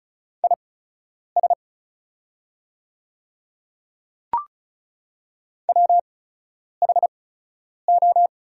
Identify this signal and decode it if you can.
Morse code, heard as a keyed tone, about 36 wpm, spaced as for 10 wpm, 700 Hz. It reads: IS WHO
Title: 8x Character Spacing - Top 100 Words - 35wpm